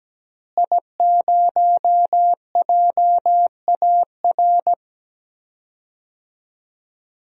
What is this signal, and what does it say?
Morse code, heard as a keyed tone, about 17 wpm, 700 Hz. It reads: I0JAR
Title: DXCC-100 Call Signs - 5 Characters - Repeated 3x - 17wpm